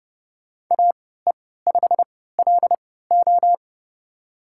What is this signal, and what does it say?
Morse code, heard as a keyed tone, about 30 wpm, spaced as for 17 wpm, 700 Hz. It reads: AE5LO
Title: USA Call Signs - 5 Characters - 3x Character Spacing - 30wpm